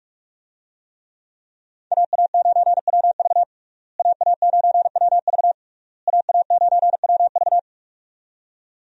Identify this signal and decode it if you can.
Morse code, heard as a keyed tone, about 44 wpm, 700 Hz. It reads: AA9WV AA9WV AA9WV
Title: USA Call Signs - 5 Characters - Repeated 3x - 45wpm